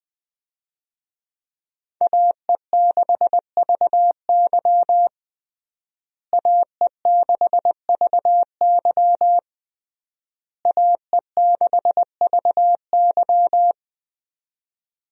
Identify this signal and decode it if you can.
Morse code, heard as a keyed tone, about 20 wpm, 700 Hz. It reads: AE6VY AE6VY AE6VY
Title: USA Call Signs - 5 Characters - Repeated 3x - 20wpm